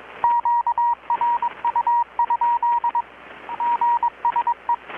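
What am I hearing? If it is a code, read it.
Morse code, heard as a keyed tone, about 22 wpm, 950 Hz, with radio noise. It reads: QRU? GSE
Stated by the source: Morse from KPH's Night of Nights XX maritime radiotelegraphy event, received in Chiapas, Mexico about 3,800 km away on a Kenwood R-600 with a grounded 10 m long-wire antenna.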